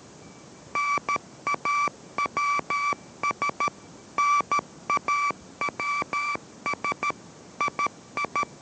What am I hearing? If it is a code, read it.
Morse code, heard as a keyed tone, about 13 wpm, 1160 Hz. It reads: NAWS NAWS II